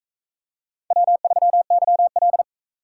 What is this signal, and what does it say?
Morse code, heard as a keyed tone, about 42 wpm, 700 Hz. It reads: W3YL